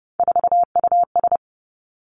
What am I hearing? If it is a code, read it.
Morse code, heard as a keyed tone, about 30 wpm, 700 Hz. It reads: 4US
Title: Three Challenging Letter-Number Combinations 30wpm